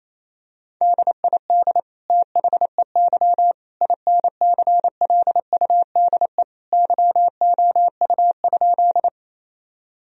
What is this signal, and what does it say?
Morse code, heard as a keyed tone, about 27 wpm, 700 Hz. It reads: DID THEY INCLUDE YOU?
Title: Sentences from Top 1000 Words - Review Part 3 - 28wpm